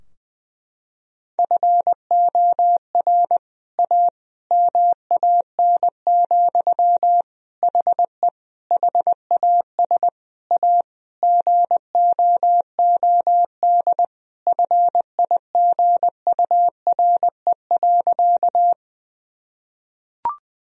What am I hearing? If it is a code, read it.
Morse code, heard as a keyed tone, about 20 wpm, 700 Hz. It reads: FOR A MAN, HE HAS A GOOD FIGURE.
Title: Sentences from Top 1000 Words - Review Part 2 - 20wpm